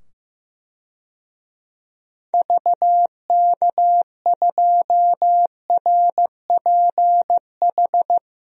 Morse code, heard as a keyed tone, about 15 wpm, 700 Hz. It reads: VK2RPH
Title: DXCC-100 Call Signs - 6 Characters - 15wpm